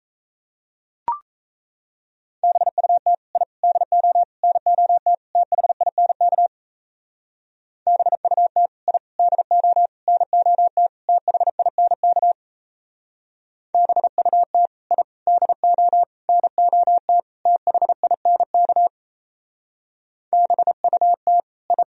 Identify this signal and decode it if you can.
Morse code, heard as a keyed tone, about 33 wpm, 700 Hz. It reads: BUT I DO NOT THINK BUT I DO NOT THINK BUT I DO NOT THINK BUT I
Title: Speed-Racing - Sets of 5 Words 28wpm